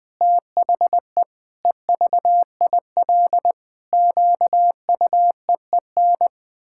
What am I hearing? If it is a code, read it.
Morse code, heard as a keyed tone, about 20 wpm, 700 Hz. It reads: THE EVIL QUEEN
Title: Pangram Challenge v2 - 20wpm